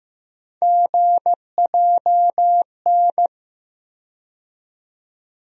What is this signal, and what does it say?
Morse code, heard as a keyed tone, about 15 wpm, 700 Hz. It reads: GJN
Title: USA Call Sign Postfixes - 3 Letters 15wpm